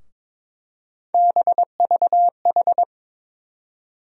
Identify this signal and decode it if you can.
Morse code, heard as a keyed tone, about 22 wpm, 700 Hz. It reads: BVH
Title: USA Call Sign Postfixes - 3 Letters 22wpm